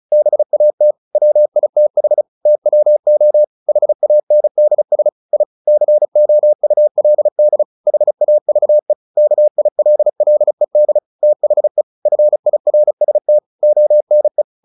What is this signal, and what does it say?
Morse code, heard as a keyed tone, about 34 wpm, 600 Hz. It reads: BAT WITH TWO HANDS I COULD HAVE KILLED THE FIRST ONE
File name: the old man and the sea 2 35wpm